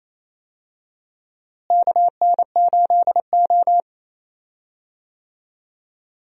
Morse code, heard as a keyed tone, about 28 wpm, 700 Hz. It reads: KN8O
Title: USA Call Signs - 4 Characters - Repeated 2x - 28wpm